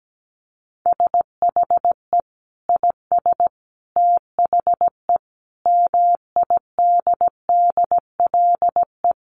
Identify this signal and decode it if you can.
Morse code, heard as a keyed tone, about 17 wpm, 700 Hz. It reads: SHE IS THE MIDDLE
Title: Sentences from Top 900 Words 17wpm